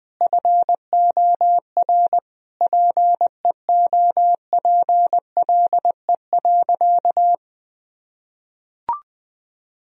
Morse code, heard as a keyed tone, about 20 wpm, 700 Hz. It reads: FOR PEOPLE.
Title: Sentences from Top 1000 Words - Review Part 3 - 20wpm